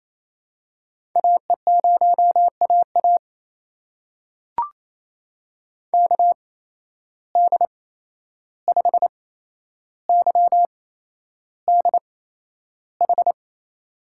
Morse code, heard as a keyed tone, about 28 wpm, spaced as for 11 wpm, 700 Hz. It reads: AE0AA KD5YDH
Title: USA Call Signs - 8x Character Spacing - 28wpm